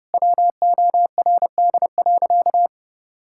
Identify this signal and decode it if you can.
Morse code, heard as a keyed tone, about 30 wpm, 700 Hz. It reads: WORD.
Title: Sentences from Top 1000 Words - Review Part 6 - 30wpm